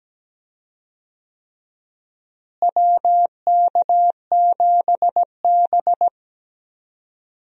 Morse code, heard as a keyed tone, about 17 wpm, 700 Hz. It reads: WK7B
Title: USA Call Signs - 4 Characters - Repeated 3x - 17wpm